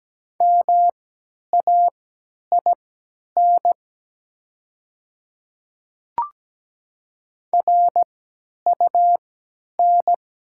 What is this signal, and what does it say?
Morse code, heard as a keyed tone, about 17 wpm, spaced as for 10 wpm, 700 Hz. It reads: MAIN RUN
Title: 3x Character Spacing - Top 300 Words - 17wpm